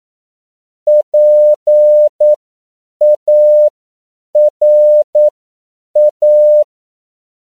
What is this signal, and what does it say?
Morse code, heard as a keyed tone, about 9 wpm, spaced as for 7 wpm, 600 Hz. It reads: PARA
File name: CW / 10PPM